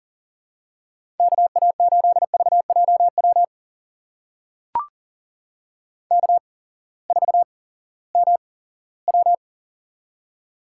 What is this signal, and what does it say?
Morse code, heard as a keyed tone, about 42 wpm, spaced as for 20 wpm, 700 Hz. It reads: KA8VJW K4MW
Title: USA Call Signs - 8x Character Spacing - 40wpm